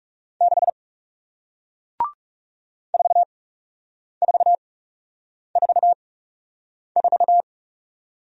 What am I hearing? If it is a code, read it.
Morse code, heard as a keyed tone, about 39 wpm, spaced as for 8 wpm, 700 Hz. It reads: 6 4444